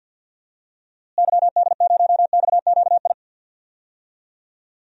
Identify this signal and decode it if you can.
Morse code, heard as a keyed tone, about 52 wpm, 700 Hz. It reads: YD0XQI
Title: DXCC-100 Call Signs - 6 Characters - Repeated 2x - 50wpm